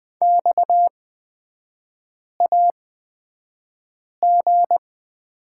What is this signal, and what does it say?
Morse code, heard as a keyed tone, about 20 wpm, spaced as for 5 wpm, 690 Hz. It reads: XAG
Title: 229 Lesson 37e - Letter x - Three Characters 20wpm